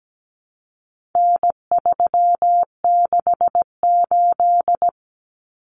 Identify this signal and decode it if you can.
Morse code, heard as a keyed tone, about 17 wpm, 700 Hz. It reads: N368
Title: Common Cut Numbers - 4-Digit 17wpm